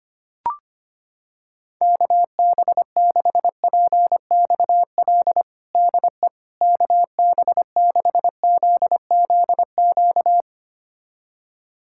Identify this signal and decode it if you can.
Morse code, heard as a keyed tone, about 25 wpm, 700 Hz. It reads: KB6PXL DE KB6ZZQ
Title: USA Callsign DE Callsign - 6 Characters - 25wpm